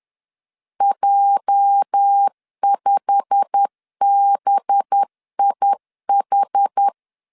Morse code, heard as a keyed tone, about 10 wpm, 800 Hz. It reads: J5BIH
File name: LEZIONI / 8 YJ45 / 8 YJ45 10wpm 2549